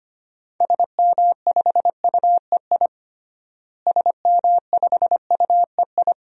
Morse code, heard as a keyed tone, about 25 wpm, 700 Hz. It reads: SM5UEI SM5UEI
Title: DXCC-100 Call Signs - 6 Characters - Repeated 2x - 25wpm